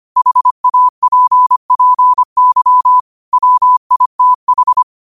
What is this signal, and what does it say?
Morse code, heard as a keyed tone, about 25 wpm, 1000 Hz. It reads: HAPPY WITH